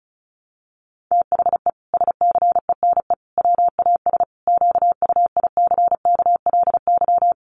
Morse code, heard as a keyed tone, about 34 wpm, 700 Hz. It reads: THE SCENE WAS QUICKLY
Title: Sentences from Top 1100 Words 35wpm